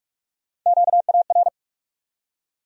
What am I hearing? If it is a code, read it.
Morse code, heard as a keyed tone, about 46 wpm, 700 Hz. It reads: QAR
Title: USA Call Sign Postfixes - 3 Letters 45wpm